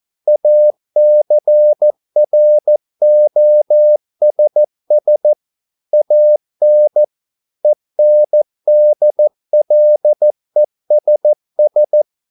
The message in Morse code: ACROSS AN ENDLESS

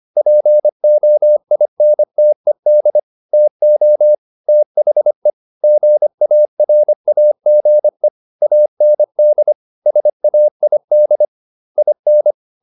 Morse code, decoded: POINTED TO THE GARAGE AND SAID IN

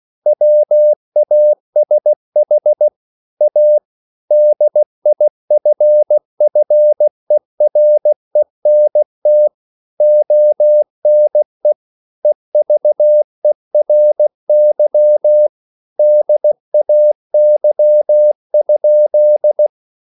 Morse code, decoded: WASH A DIFFERENT ONE EVERY DAY?

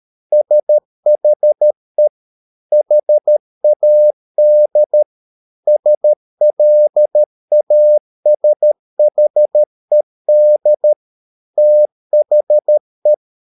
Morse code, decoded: SHE HAD SLASHED THE

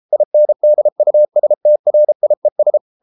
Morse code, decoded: INDUSTRIES

